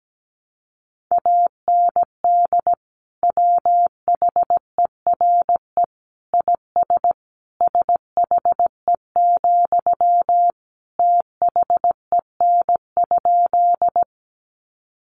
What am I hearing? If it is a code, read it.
Morse code, heard as a keyed tone, about 17 wpm, 700 Hz. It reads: AND WHERE IS SHE, THEN?
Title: Sentences from Top 200 Words 17wpm